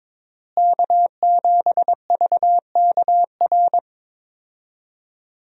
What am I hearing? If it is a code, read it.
Morse code, heard as a keyed tone, about 22 wpm, 700 Hz. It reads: K7VKR
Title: USA Call Signs - 5 Characters - Repeated 2x - 22wpm